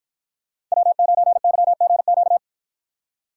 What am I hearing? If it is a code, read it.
Morse code, heard as a keyed tone, about 53 wpm, 700 Hz. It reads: W9YGQ